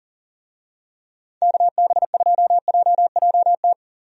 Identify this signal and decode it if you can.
Morse code, heard as a keyed tone, about 40 wpm, 700 Hz. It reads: KB2JJT
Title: USA Call Signs - 6 Characters - 40wpm